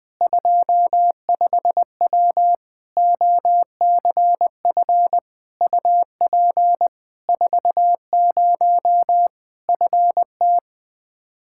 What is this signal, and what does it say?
Morse code, heard as a keyed tone, about 21 wpm, 700 Hz. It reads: 25W OCF UP 40 FT
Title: Practice QSO - Exchange 2 - Info - 20wpm